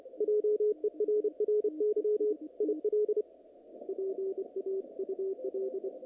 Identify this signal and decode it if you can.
Morse code, heard as a keyed tone, about 28 wpm, 440 Hz, with radio noise. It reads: JERRY IL PAUL
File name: contests / naqp-cw-k5tr-op-k5pi- -2300